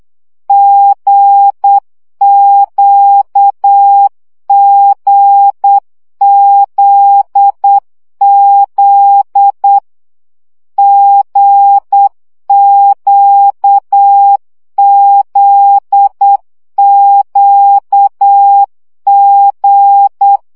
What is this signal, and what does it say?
Morse code, heard as a keyed tone, about 8 wpm, 800 Hz. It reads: GQGZZ GQZQG
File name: CW gyakorlas g z q 7 baud